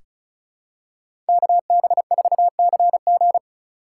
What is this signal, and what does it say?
Morse code, heard as a keyed tone, about 36 wpm, 700 Hz. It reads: KB4CG
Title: K1USN SST QSO Practice - 35wpm